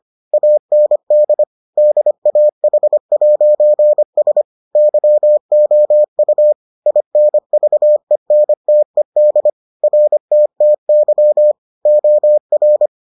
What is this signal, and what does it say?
Morse code, heard as a keyed tone, about 25 wpm, 600 Hz. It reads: AND DAH'S YOU INVENTED RTTY OR